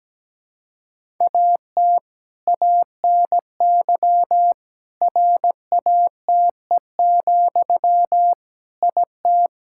Morse code, heard as a keyed tone, about 17 wpm, 700 Hz. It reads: AT ANY RATE, IT
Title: Sentences from Top 1600 Words 17wpm